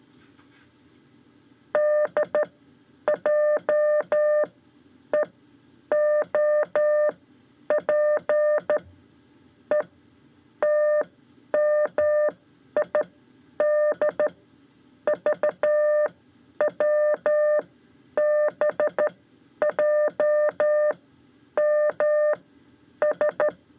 Morse code, heard as a keyed tone, about 13 wpm, spaced as for 10 wpm, 600 Hz. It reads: D J E O P E TMI D VW BJ M S
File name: 5 Lezione CW 2